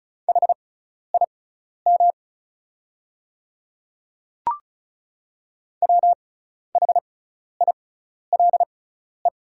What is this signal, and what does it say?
Morse code, heard as a keyed tone, about 34 wpm, spaced as for 12 wpm, 690 Hz. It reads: HIM WHILE
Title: 6x Character Spacing - Top 300 Words - 35wpm